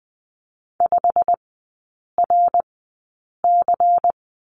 Morse code, heard as a keyed tone, about 20 wpm, spaced as for 8 wpm, 700 Hz. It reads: 5RC
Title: 178 Lesson 29e - Slash - Three Characters 20wpm